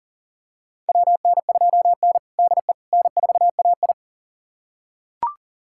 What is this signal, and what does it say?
Morse code, heard as a keyed tone, about 40 wpm, 700 Hz. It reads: WN2N DE N4AI
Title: USA Callsign DE Callsign - 4 Characters - 40wpm